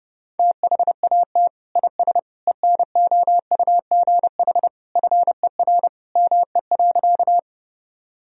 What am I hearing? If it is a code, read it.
Morse code, heard as a keyed tone, about 30 wpm, 700 Hz. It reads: THAT IS ENOUGH FER ME.
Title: Sentences from Top 400 Words and Abbreviations 30wpm